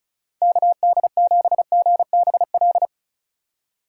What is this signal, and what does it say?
Morse code, heard as a keyed tone, about 36 wpm, 700 Hz. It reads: KD7GBL
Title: USA Call Signs - 6 Characters - 3x Character Spacing - 35wpm